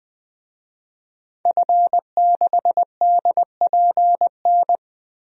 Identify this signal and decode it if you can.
Morse code, heard as a keyed tone, about 20 wpm, 700 Hz. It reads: F6DPN
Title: DXCC-100 Call Signs - 5 Characters - 20wpm